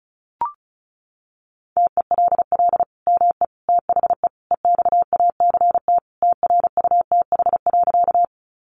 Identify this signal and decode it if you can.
Morse code, heard as a keyed tone, about 34 wpm, 700 Hz. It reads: TELL ME THE EXACT TRUTH.
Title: Sentences from Top 1300 Words 35wpm